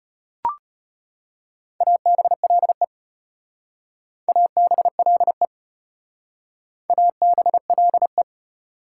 Morse code, heard as a keyed tone, about 32 wpm, 700 Hz. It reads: ABLE ABLE ABLE